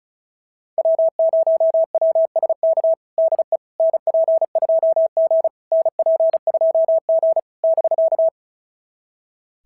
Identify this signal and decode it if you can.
Morse code, heard as a keyed tone, about 34 wpm, 660 Hz. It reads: W0WSK DE NP2G NP2G <BK>
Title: SOTA QSO Practice 35wpm